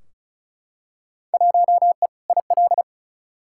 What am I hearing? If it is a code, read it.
Morse code, heard as a keyed tone, about 36 wpm, 700 Hz. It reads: 1E IL
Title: Field Day Exchanges - 35wpm